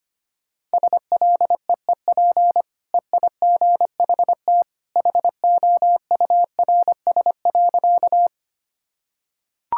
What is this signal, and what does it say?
Morse code, heard as a keyed tone, about 25 wpm, 700 Hz. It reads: SLEEP EIGHT HOURS.
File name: Easy Sentences with Hints - 25wpm